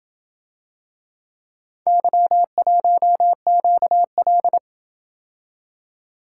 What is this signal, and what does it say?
Morse code, heard as a keyed tone, about 28 wpm, 700 Hz. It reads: Y1QL